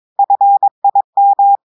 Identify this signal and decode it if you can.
Morse code, heard as a keyed tone, about 22 wpm, 800 Hz. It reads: FIM
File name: audios / portugais / portugais texte en clair 2 vitesse 22 mots minutes